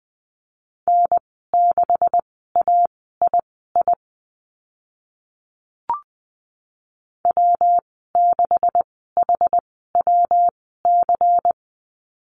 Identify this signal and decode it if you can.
Morse code, heard as a keyed tone, about 20 wpm, spaced as for 14 wpm, 700 Hz. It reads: N6AII W6HWC
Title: 140 Lesson 23g - Letter f - Call Signs 20wpm